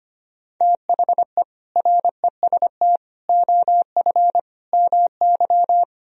Morse code, heard as a keyed tone, about 32 wpm, 700 Hz. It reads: THE REST OF MY